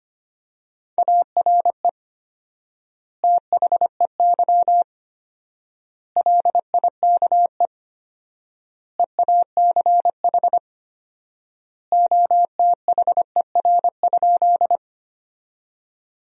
Morse code, heard as a keyed tone, about 25 wpm, 700 Hz. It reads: ARE THEY LIKE EACH OTHER?
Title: 4x Word-Spacing - Sentences from Top 200 Words - 25wpm